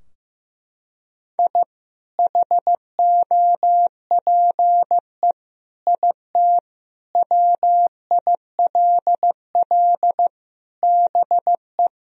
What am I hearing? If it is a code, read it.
Morse code, heard as a keyed tone, about 16 wpm, 700 Hz. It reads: I HOPE IT WILL BE
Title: Sentences from Top 1000 Words - Review Part 2 - 15wpm